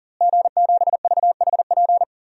Morse code, heard as a keyed tone, about 38 wpm, 700 Hz. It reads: G7VHP
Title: DXCC-100 Call Signs - 5 Characters - Repeated 2x - 40wpm